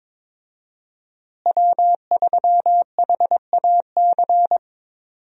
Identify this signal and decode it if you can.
Morse code, heard as a keyed tone, about 22 wpm, 690 Hz. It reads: W3HAC
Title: USA Call Signs - 5 Characters - Repeated 3x - 22wpm